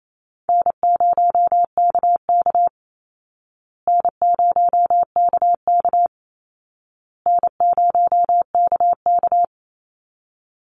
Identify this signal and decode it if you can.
Morse code, heard as a keyed tone, about 28 wpm, 700 Hz. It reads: N0KK N0KK N0KK
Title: Easy Call Signs Repeated 3x - 28wpm